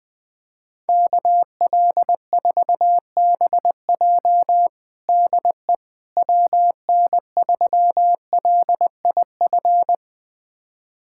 Morse code, heard as a keyed tone, about 21 wpm, 700 Hz. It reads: KL4BJ DE WN3LIF
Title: USA Callsign DE Callsign 20wpm